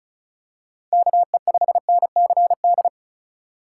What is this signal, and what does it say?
Morse code, heard as a keyed tone, about 34 wpm, 700 Hz. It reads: KE5NCD
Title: USA Call Signs - 6 Characters - 2x Character Spacing - 35wpm